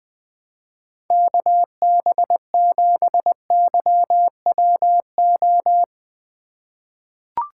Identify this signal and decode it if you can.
Morse code, heard as a keyed tone, about 20 wpm, 700 Hz. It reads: KB7YWO
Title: USA Call Signs - 6 Characters - 3x Character Spacing - 20wpm